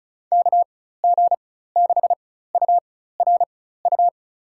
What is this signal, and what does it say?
Morse code, heard as a keyed tone, about 34 wpm, spaced as for 16 wpm, 700 Hz. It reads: KG6URU
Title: USA Call Signs - 6 Characters - 4x Character Spacing - 35wpm